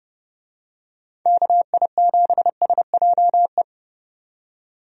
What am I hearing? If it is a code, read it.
Morse code, heard as a keyed tone, about 31 wpm, 700 Hz. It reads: KI7SJE
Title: USA Call Signs - 6 Characters - 30wpm